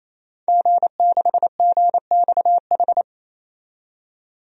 Morse code, heard as a keyed tone, about 28 wpm, 700 Hz. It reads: G6GXH
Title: DXCC-100 Call Signs - 5 Characters - 28wpm